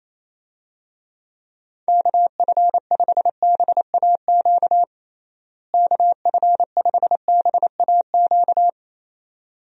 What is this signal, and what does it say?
Morse code, heard as a keyed tone, about 28 wpm, 700 Hz. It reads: KF5BAQ KF5BAQ
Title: USA Call Signs - 6 Characters - Repeated 2x - 28wpm